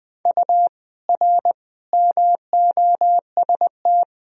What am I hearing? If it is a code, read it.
Morse code, heard as a keyed tone, about 20 wpm, 700 Hz. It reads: U R MOST
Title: Sentences from Top 200 Words and Abbreviations 20wpm